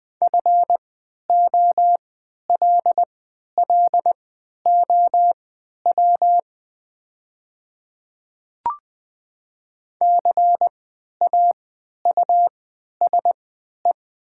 Morse code, heard as a keyed tone, about 20 wpm, spaced as for 11 wpm, 700 Hz. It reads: FOLLOW CAUSE
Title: 3x Character Spacing - Top 300 Words - 20wpm